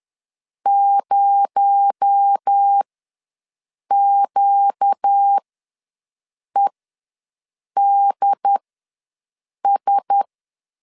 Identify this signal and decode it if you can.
Morse code, heard as a keyed tone, about 10 wpm, spaced as for 6 wpm, 800 Hz. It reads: 0QEDS
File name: LEZIONI / 6 PKX0 / 6 PKX0 ws 10wpm 4220